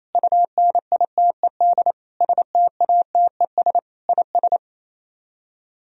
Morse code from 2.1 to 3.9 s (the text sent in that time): STATES